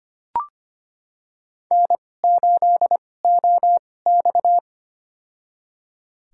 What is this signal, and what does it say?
Morse code, heard as a keyed tone, about 25 wpm, spaced as for 18 wpm, 700 Hz. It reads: N8OX
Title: USA Call Signs - 4 Characters - 2x Character Spacing - 25wpm